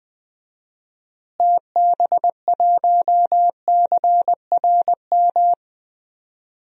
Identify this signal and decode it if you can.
Morse code, heard as a keyed tone, about 20 wpm, 700 Hz. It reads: TB1CRM
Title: DXCC-100 Call Signs - 6 Characters - Repeated 3x - 20wpm